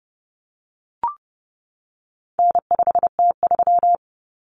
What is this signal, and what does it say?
Morse code, heard as a keyed tone, about 30 wpm, 700 Hz. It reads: N5T3
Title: Common Cut Numbers - 4-Digit 30wpm